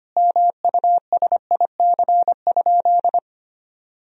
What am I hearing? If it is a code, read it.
Morse code, heard as a keyed tone, about 25 wpm, 700 Hz. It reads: MUSIC?
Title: Sentences from Top 1000 Words - Review Part 1 - 25wpm